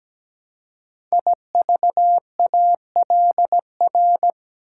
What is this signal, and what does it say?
Morse code, heard as a keyed tone, about 17 wpm, 700 Hz. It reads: IVALR
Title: Code Groups - 5 Letters - 17wpm